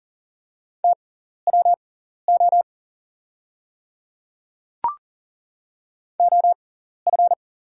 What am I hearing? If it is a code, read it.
Morse code, heard as a keyed tone, about 40 wpm, spaced as for 14 wpm, 700 Hz. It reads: TWO OF
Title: 6x Character Spacing - Top 100 Words - 40wpm